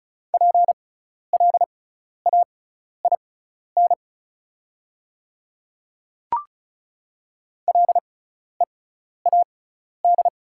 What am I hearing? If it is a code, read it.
Morse code, heard as a keyed tone, about 34 wpm, spaced as for 12 wpm, 690 Hz. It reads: PLAIN LEAD
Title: 6x Character Spacing - Top 500 Words - 35wpm